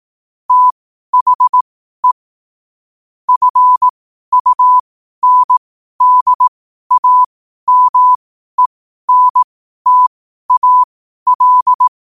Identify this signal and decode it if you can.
Morse code, heard as a keyed tone, about 18 wpm, spaced as for 12 wpm, 1000 Hz. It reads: THE FUNDAMENTAL
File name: qotdfiles / 12wpm